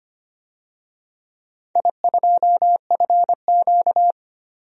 Morse code, heard as a keyed tone, about 25 wpm, 700 Hz. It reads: I2FQ
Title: DXCC-100 Call Signs - 4 Characters - Repeated 3x - 25wpm